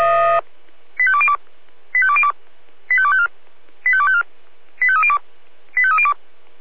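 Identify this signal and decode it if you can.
Five-tone selective call (ZVEI), heard as quick runs of five tones, tones of about 70 ms. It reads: 85202 85202 85254 85254 85302 85302
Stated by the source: ZVEI five-tone call received by SDR.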